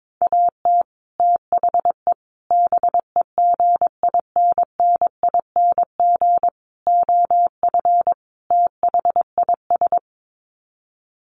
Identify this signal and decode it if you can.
Morse code, heard as a keyed tone, about 22 wpm, 700 Hz. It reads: AT THE BEGINNING OF THIS
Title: Sets of 5 Words Encore 22wpm